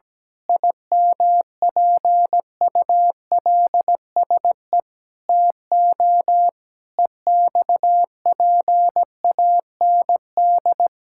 Morse code, heard as a keyed tone, about 17 wpm, 700 Hz. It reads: IMPULSE TO EXPAND
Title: Pangram Challenge v2 - 17wpm